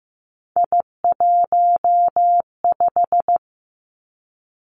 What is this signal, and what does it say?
Morse code, heard as a keyed tone, about 15 wpm, 700 Hz. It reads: I15
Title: Three Challenging Letter-Number Combinations 15wpm